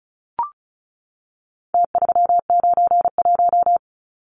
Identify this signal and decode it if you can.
Morse code, heard as a keyed tone, about 34 wpm, 700 Hz. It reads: T391
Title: Numbers - Contest Style - 4-Digit 35wpm